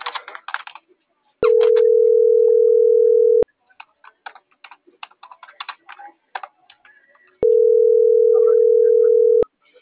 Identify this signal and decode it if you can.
Ringback tone (North America), 440 + 480 Hz.